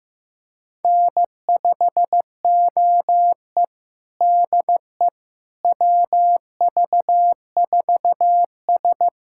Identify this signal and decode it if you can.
Morse code, heard as a keyed tone, about 15 wpm, 700 Hz. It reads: N5OE DE WV4S
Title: USA Callsign DE Callsign - 4 Characters - 15wpm